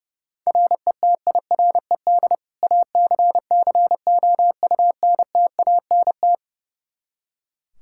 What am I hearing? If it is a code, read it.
Morse code, heard as a keyed tone, about 30 wpm, 700 Hz. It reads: RETIRED ACCOUNTANT
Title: K1USN SST QSO Practice 30wpm